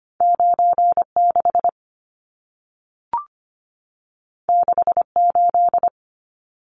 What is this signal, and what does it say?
Morse code, heard as a keyed tone, about 25 wpm, 700 Hz. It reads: 96 68
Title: Numbers - 2-Digit 25wpm